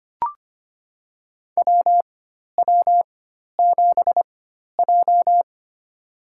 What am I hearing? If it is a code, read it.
Morse code, heard as a keyed tone, about 25 wpm, spaced as for 12 wpm, 700 Hz. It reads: WW7J
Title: USA Call Signs - 4 Characters - 4x Character Spacing - 25wpm